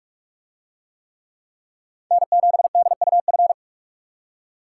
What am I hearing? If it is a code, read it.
Morse code, heard as a keyed tone, about 44 wpm, 700 Hz. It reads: N7DUF